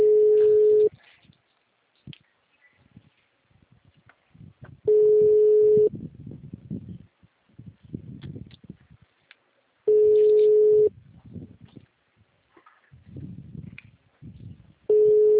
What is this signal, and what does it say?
Ringback tone (Europe, 425 Hz).